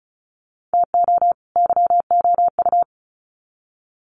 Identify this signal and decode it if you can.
Morse code, heard as a keyed tone, about 34 wpm, 700 Hz. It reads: TO YOU
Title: Sets of 2 Words 35wpm